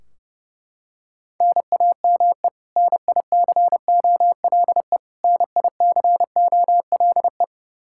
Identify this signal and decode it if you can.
Morse code, heard as a keyed tone, about 32 wpm, 700 Hz. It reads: NAME NICOLE NICOLE
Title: Practice QSO - Exchange 1 - Beginning - 30wpm